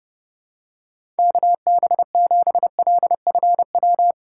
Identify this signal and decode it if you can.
Morse code, heard as a keyed tone, about 30 wpm, 700 Hz. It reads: KB7LFW